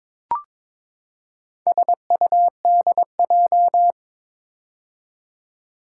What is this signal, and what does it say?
Morse code, heard as a keyed tone, about 22 wpm, 700 Hz. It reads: SUDJ
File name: practice / 22wpm / Four Letter-Number Combinations - 22wpm